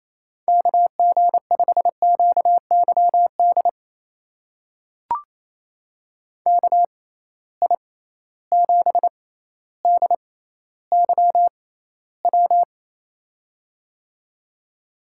Morse code, heard as a keyed tone, about 28 wpm, spaced as for 14 wpm, 700 Hz. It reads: KG5QYD KI7DYW
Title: USA Call Signs - 6 Characters - 6x Character Spacing - 28wpm